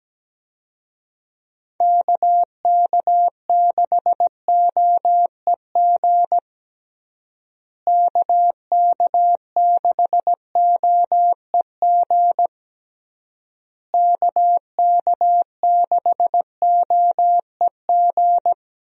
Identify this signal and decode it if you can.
Morse code, heard as a keyed tone, about 17 wpm, 700 Hz. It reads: KK6OEG KK6OEG KK6OEG
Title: USA Call Signs - 6 Characters - Repeated 3x - 17wpm